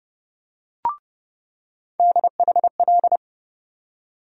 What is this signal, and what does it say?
Morse code, heard as a keyed tone, about 30 wpm, 700 Hz. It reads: DHL
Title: USA Call Sign Postfixes - 3 Letters 30wpm